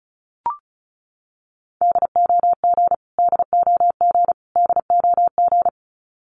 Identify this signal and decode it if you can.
Morse code, heard as a keyed tone, about 34 wpm, 700 Hz. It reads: DOG DOG DOG
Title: Top 500 Words Repeated 35wpm